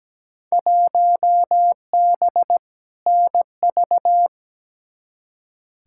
Morse code, heard as a keyed tone, about 17 wpm, 700 Hz. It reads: 1B NV
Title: Field Day Exchanges - 17wpm